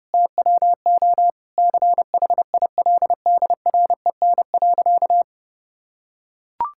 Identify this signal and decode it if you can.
Morse code, heard as a keyed tone, about 30 wpm, 700 Hz. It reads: TWO CHILDREN.
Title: Sentences from Top 1000 Words - Review Part 5 - 30wpm